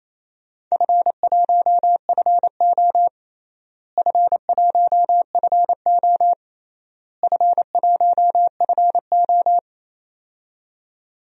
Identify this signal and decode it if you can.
Morse code, heard as a keyed tone, about 28 wpm, 700 Hz. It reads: F1FO F1FO F1FO
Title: DXCC-100 Call Signs - 4 Characters - Repeated 3x - 28wpm